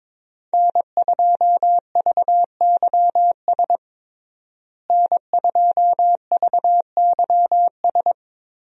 Morse code, heard as a keyed tone, about 22 wpm, 700 Hz. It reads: N2VYS N2VYS
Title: USA Call Signs - 5 Characters - Repeated 2x - 22wpm